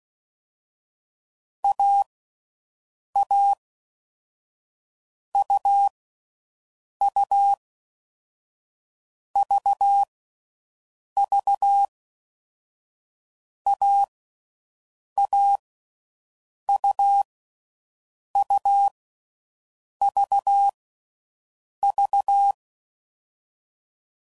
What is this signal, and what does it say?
Morse code, heard as a keyed tone, about 16 wpm, 800 Hz. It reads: A A U U V V A A U U V V
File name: Mixed / Short Runs / AUV AUV4